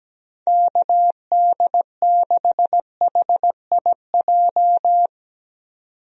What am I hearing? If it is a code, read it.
Morse code, heard as a keyed tone, about 17 wpm, 700 Hz. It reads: KD6HIJ
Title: USA Call Signs - 6 Characters - 17wpm